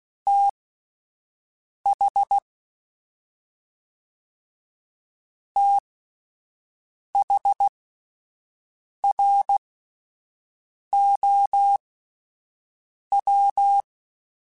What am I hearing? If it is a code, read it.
Morse code, heard as a keyed tone, about 16 wpm, spaced as for 5 wpm, 800 Hz. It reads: TH THROW
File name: THs / TH / TH Prefixes 1x1 @5